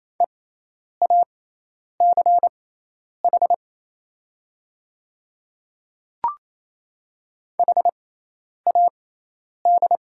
Morse code, heard as a keyed tone, about 28 wpm, spaced as for 10 wpm, 700 Hz. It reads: EACH HAD
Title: 6x Character Spacing - Top 100 Words - 28wpm